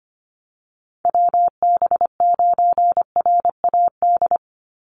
Morse code, heard as a keyed tone, about 25 wpm, 700 Hz. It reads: WB9RAD
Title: USA Call Signs 25wpm